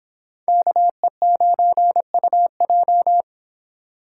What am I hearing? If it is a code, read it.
Morse code, heard as a keyed tone, about 25 wpm, 700 Hz. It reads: KE9UJ